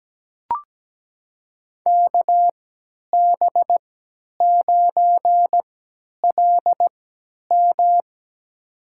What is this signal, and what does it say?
Morse code, heard as a keyed tone, about 17 wpm, spaced as for 10 wpm, 700 Hz. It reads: KB9LM
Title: USA Call Signs - 5 Characters - 3x Character Spacing - 17wpm